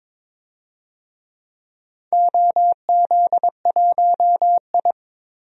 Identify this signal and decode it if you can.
Morse code, heard as a keyed tone, about 22 wpm, 700 Hz. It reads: OZ1I
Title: DXCC-100 Call Signs - 4 Characters - 22wpm